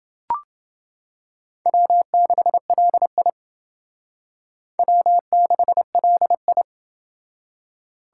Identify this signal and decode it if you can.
Morse code, heard as a keyed tone, about 28 wpm, 700 Hz. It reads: W6LI W6LI